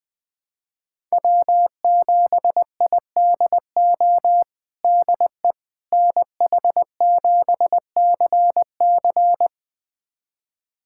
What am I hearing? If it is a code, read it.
Morse code, heard as a keyed tone, about 21 wpm, 700 Hz. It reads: W7IDO DE NH7CC
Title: USA Callsign DE Callsign - 5 Characters - 20wpm